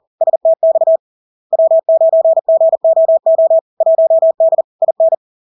Morse code, heard as a keyed tone, about 40 wpm, 650 Hz. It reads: STX W9GOO 1D IN
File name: LARK-FD-2022-Log-in-CW-40wpm-650-Hz